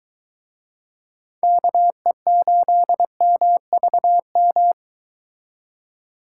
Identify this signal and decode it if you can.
Morse code, heard as a keyed tone, about 24 wpm, 700 Hz. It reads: KE8MVM